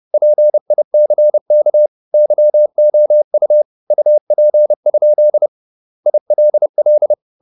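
Morse code, heard as a keyed tone, about 30 wpm, 600 Hz. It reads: PICK YOU UP? ILL
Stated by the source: Morse code at 30 wpm, the novel Never by Ken Follett in English, from CW med Gnister.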